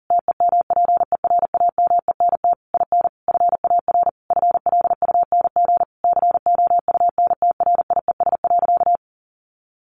Morse code, heard as a keyed tone, about 40 wpm, 700 Hz. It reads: TEMPERAMENT IN FAR FLUNG COUNTRIES.
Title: A Sound Of Thunder 40wpm